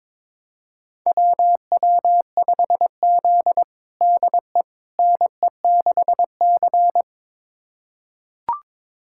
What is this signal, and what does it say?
Morse code, heard as a keyed tone, about 22 wpm, 700 Hz. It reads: WW5Z DE NE6C
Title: USA Callsign DE Callsign - 4 Characters - 22wpm